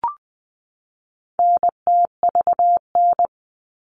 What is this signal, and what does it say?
Morse code, heard as a keyed tone, about 20 wpm, 700 Hz. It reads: NTVN
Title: All Cut Numbers - 4-Digit 20wpm